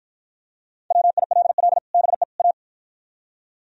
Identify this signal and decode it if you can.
Morse code, heard as a keyed tone, about 52 wpm, 700 Hz. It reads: WILL BE A